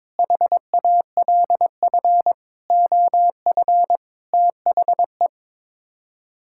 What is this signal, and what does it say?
Morse code, heard as a keyed tone, about 21 wpm, 700 Hz. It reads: HALF OF THE
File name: practice / 22wpm / Sets of 6 Words - 22wpm